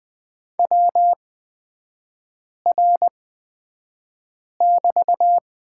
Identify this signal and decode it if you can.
Morse code, heard as a keyed tone, about 20 wpm, spaced as for 5 wpm, 700 Hz. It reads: WR<BT>
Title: 250 Lesson 40e - BK - Three Characters 20wpm